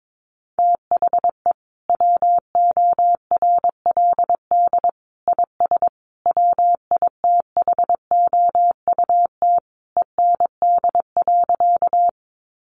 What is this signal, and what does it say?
Morse code, heard as a keyed tone, about 22 wpm, 700 Hz. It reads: THE WORLD IS WITHOUT END.
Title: Sentences from Top 600 Words 22wpm